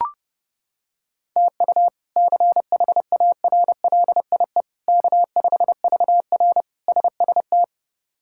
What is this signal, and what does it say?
Morse code, heard as a keyed tone, about 30 wpm, 700 Hz. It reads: TU CHARLIE K5VR SST
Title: K1USN SST QSO Practice - 30wpm